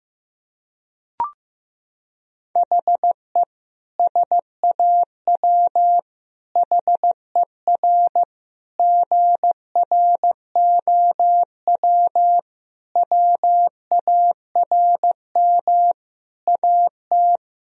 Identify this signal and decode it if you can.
Morse code, heard as a keyed tone, about 16 wpm, 700 Hz. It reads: HE SAW HER GROW WARM AT
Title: Sentences from Top 1000 Words - Review Part 2 - 15wpm